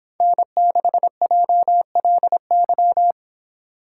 Morse code, heard as a keyed tone, about 25 wpm, 700 Hz. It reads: N6JLY